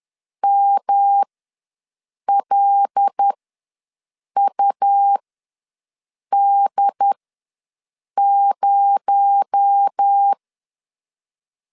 Morse code, heard as a keyed tone, about 10 wpm, spaced as for 6 wpm, 800 Hz. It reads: MLUD0
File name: LEZIONI / 7 GF9Z / 7 GF9Z ws 10wpm 3919